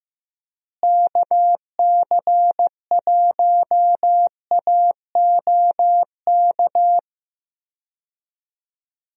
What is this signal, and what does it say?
Morse code, heard as a keyed tone, about 16 wpm, 700 Hz. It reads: KC1AOK
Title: USA Call Signs - 6 Characters - Repeated 2x - 15wpm